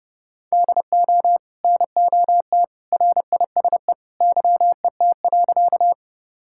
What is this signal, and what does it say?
Morse code, heard as a keyed tone, about 30 wpm, 700 Hz. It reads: DO NOT RISE YET.
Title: Sentences from Top 1000 Words - Review Part 6 - 30wpm